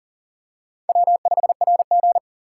Morse code, heard as a keyed tone, about 40 wpm, 700 Hz. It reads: W5RG
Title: USA Call Signs - 4 Characters - 6x Character Spacing - 40wpm